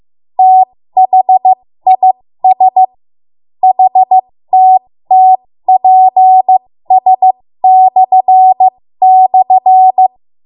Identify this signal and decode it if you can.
Morse code, heard as a keyed tone, about 15 wpm, spaced as for 13 wpm, 750 Hz. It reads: THIS HTTPS//
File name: Morse / 13WPM